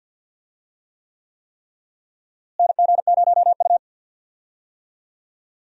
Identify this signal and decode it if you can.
Morse code, heard as a keyed tone, about 50 wpm, 700 Hz. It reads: NG0U
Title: USA Call Signs - 4 Characters - Repeated 3x - 50wpm